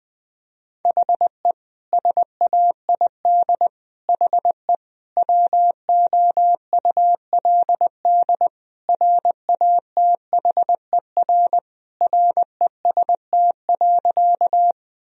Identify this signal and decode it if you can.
Morse code, heard as a keyed tone, about 20 wpm, 700 Hz. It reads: HE SAID HE WOULD RATHER REST.
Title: Sentences from Top 1000 Words - Review Part 4 - 20wpm